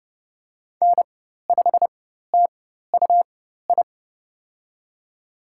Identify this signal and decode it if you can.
Morse code, heard as a keyed tone, about 29 wpm, spaced as for 14 wpm, 700 Hz. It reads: N5TUI
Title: USA Call Signs - 5 Characters - 4x Character Spacing - 30wpm